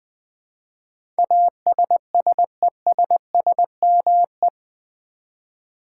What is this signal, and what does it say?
Morse code, heard as a keyed tone, about 20 wpm, 700 Hz. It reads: ASSESSME
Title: Build A Word - 4 Letters At A Time - 10-Letter Words - 20wpm